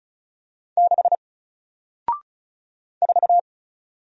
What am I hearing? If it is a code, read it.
Morse code, heard as a keyed tone, about 36 wpm, 700 Hz. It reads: 6 4
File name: Single-Letter-Number-35wpm